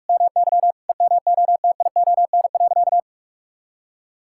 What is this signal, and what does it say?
Morse code, heard as a keyed tone, about 41 wpm, 700 Hz. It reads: MY EMOTION.
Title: Sentences from Top 1600 Words 45wpm